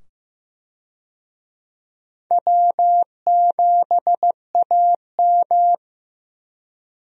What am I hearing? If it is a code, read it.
Morse code, heard as a keyed tone, about 15 wpm, 700 Hz. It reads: W7AM
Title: USA Call Signs - 4 Characters - 15wpm